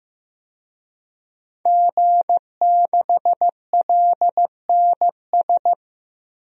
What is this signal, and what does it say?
Morse code, heard as a keyed tone, about 15 wpm, 700 Hz. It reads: G6LNS
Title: DXCC-100 Call Signs - 5 Characters - 15wpm